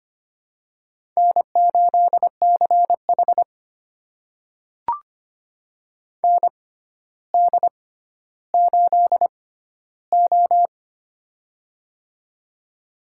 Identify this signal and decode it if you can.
Morse code, heard as a keyed tone, about 25 wpm, spaced as for 13 wpm, 700 Hz. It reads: N8CH ND8O
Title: USA Call Signs - 4 Characters - 6x Character Spacing - 25wpm